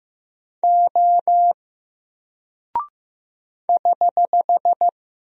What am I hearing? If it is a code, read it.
Morse code, heard as a keyed tone, about 15 wpm, 700 Hz. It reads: O <HH>
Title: Single Character (Letter:Number:Punctuation:Prosign) 15wpm